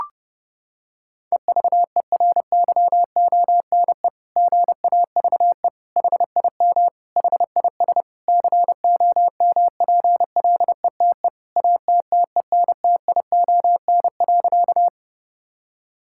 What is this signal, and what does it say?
Morse code, heard as a keyed tone, about 32 wpm, 700 Hz. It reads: EVERYONE GAVE HIM HIS COMPLETE ATTENTION.
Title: Sentences from Top 1000 Words - Review Part 6 - 30wpm